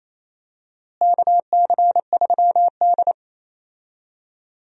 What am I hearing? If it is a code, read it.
Morse code, heard as a keyed tone, about 27 wpm, 700 Hz. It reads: KC3D
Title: USA Call Signs - 4 Characters - Repeated 3x - 28wpm